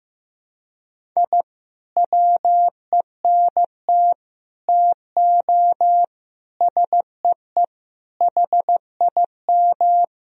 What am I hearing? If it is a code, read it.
Morse code, heard as a keyed tone, about 15 wpm, 700 Hz. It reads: I WENT TO SEE HIM
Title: Sentences from Top 1000 Words - Review Part 1 - 15wpm